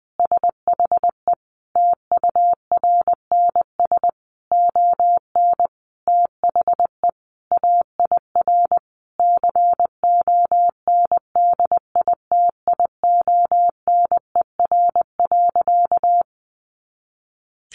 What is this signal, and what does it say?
Morse code, heard as a keyed tone, about 20 wpm, 700 Hz. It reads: SHE TURNS ON THE AIR CONDITIONER.
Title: Additional Easy Phrases 20wpm